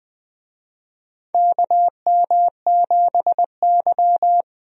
Morse code, heard as a keyed tone, about 20 wpm, 700 Hz. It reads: KM7Y
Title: USA Call Signs - 4 Characters - 20wpm